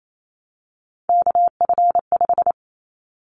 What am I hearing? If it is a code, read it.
Morse code, heard as a keyed tone, about 28 wpm, 700 Hz. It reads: KF5
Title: Callsign Prefixes for Top 25 DXCC - 28wpm